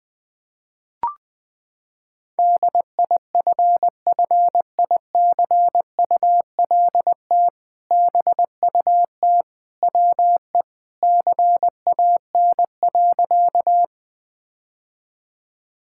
Morse code, heard as a keyed tone, about 20 wpm, 700 Hz. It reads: DIFFICULT BUT WE CAN.
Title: Sentences with 2-Word Phrases - 20wpm